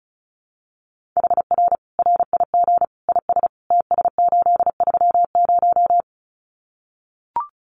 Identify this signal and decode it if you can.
Morse code, heard as a keyed tone, about 34 wpm, 700 Hz. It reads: HR RIG IS TS830
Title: Common QSO Phrases 35wpm